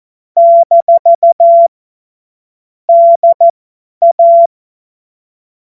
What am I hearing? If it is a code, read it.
Morse code, heard as a keyed tone, about 14 wpm, spaced as for 10 wpm, 680 Hz. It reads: - DA
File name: Morsekurs / Witze / Witz 23. Mai 2023 50 F70